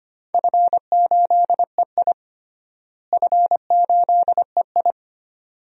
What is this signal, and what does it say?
Morse code, heard as a keyed tone, about 25 wpm, 700 Hz. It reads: F8EI F8EI
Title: DXCC-100 Call Signs - 4 Characters - Repeated 2x - 25wpm